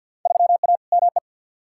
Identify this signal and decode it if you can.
Morse code, heard as a keyed tone, about 46 wpm, 700 Hz. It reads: 3A ME
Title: Field Day Exchanges - 50wpm